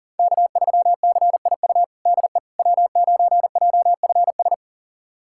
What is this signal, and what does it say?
Morse code, heard as a keyed tone, about 40 wpm, 700 Hz. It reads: K3CIU DE W9JFS
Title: USA Callsign DE Callsign - 5 Characters - 40wpm